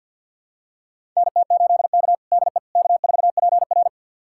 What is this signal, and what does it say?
Morse code, heard as a keyed tone, about 46 wpm, 700 Hz. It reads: NT8K DE K4PR
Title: USA Callsign DE Callsign - 4 Characters - 50wpm